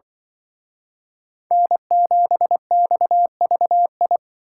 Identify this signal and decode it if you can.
Morse code, heard as a keyed tone, about 24 wpm, 700 Hz. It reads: N7XVI
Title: Speed Racing - USA 1x3 Call Signs 20wpm